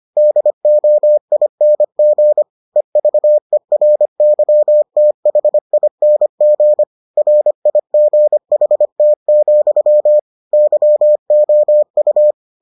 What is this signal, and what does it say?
Morse code, heard as a keyed tone, about 25 wpm, 600 Hz. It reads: DOING EVERYTHING RIGHT, YOU